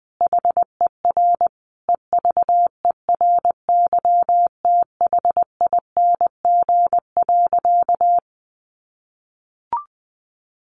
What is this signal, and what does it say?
Morse code, heard as a keyed tone, about 20 wpm, 700 Hz. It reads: HER EVERYTHING.
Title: Sentences from Top 700 Words 20wpm